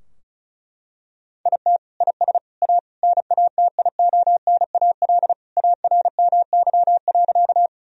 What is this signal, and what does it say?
Morse code, heard as a keyed tone, about 36 wpm, 700 Hz. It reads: IT IS A NATIONAL ARMY.
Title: Sentences from Top 1000 Words - Review Part 4 - 35wpm